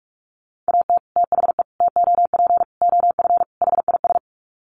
Morse code, heard as a keyed tone, about 44 wpm, 700 Hz. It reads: AT THE TOP OF HIS
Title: Sets of 5 Words Encore 45wpm